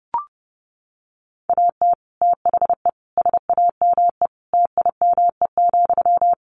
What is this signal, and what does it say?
Morse code, heard as a keyed tone, about 30 wpm, 700 Hz. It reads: AT THE SAME TIME,
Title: Sentences from Top 1500 Words 30wpm